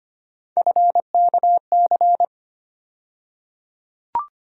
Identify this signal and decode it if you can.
Morse code, heard as a keyed tone, about 25 wpm, 700 Hz. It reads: FKC
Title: USA Call Sign Postfixes - 3 Letters 25wpm